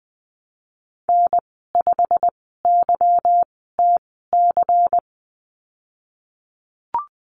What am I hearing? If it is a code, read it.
Morse code, heard as a keyed tone, about 20 wpm, spaced as for 15 wpm, 700 Hz. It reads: N5YTC
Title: 193 Lesson 31g - Letter v - Call Signs 20wpm